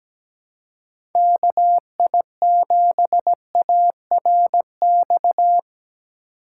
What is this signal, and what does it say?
Morse code, heard as a keyed tone, about 17 wpm, 700 Hz. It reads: KI7ARX
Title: USA Call Signs - 6 Characters - 3x Character Spacing - 17wpm